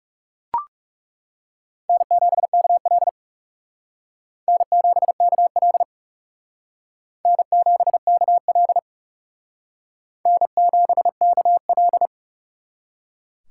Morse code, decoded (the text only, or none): N7KL N7KL N7KL N7KL